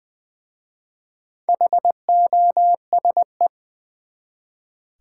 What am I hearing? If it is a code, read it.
Morse code, heard as a keyed tone, about 20 wpm, 690 Hz. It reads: HOSE
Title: Easy Words with Hints - 20wpm